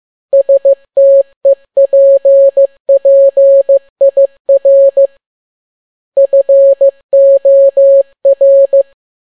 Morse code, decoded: STEPPIR FOR